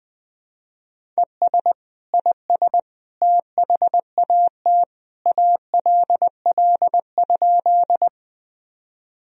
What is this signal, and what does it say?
Morse code, heard as a keyed tone, about 21 wpm, 700 Hz. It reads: ES IS THAT ALL?
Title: Sentences from Top 100 Words and Abbreviations 20wpm